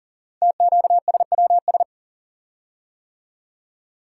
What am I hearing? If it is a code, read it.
Morse code, heard as a keyed tone, about 41 wpm, 700 Hz. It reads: TQSWS
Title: Code Groups - 5 Letters - 40wpm